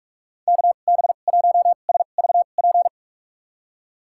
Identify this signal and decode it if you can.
Morse code, heard as a keyed tone, about 46 wpm, spaced as for 32 wpm, 700 Hz. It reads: KB1SVP
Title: USA Call Signs - 6 Characters - 2x Character Spacing - 45wpm